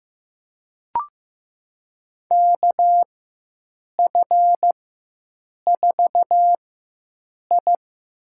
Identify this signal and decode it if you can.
Morse code, heard as a keyed tone, about 15 wpm, spaced as for 7 wpm, 700 Hz. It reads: KF4I
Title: USA Call Signs - 4x Character Spacing - 15wpm